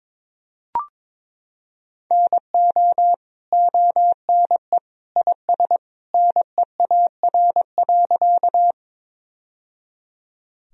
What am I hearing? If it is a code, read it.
Morse code, heard as a keyed tone, about 22 wpm, 700 Hz. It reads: NO ONE IS NEAR.
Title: Sentences from Top 1000 Words - Review Part 1 - 22wpm